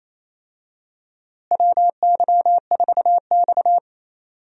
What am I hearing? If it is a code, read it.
Morse code, heard as a keyed tone, about 28 wpm, 700 Hz. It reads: WY4X
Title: USA Call Signs - 4 Characters - Repeated 2x - 28wpm